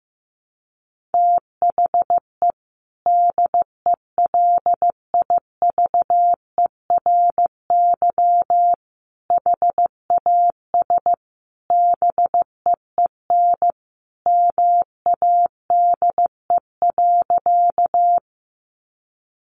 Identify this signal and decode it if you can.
Morse code, heard as a keyed tone, about 15 wpm, 700 Hz. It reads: THE DELIVERY HAS BEEN MADE.
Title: Sentences from Top 1500 Words 15wpm